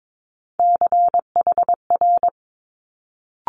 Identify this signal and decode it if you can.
Morse code, heard as a keyed tone, about 22 wpm, 700 Hz. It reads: CHR
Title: Multi-letter Phonemes Quick Fire 22wpm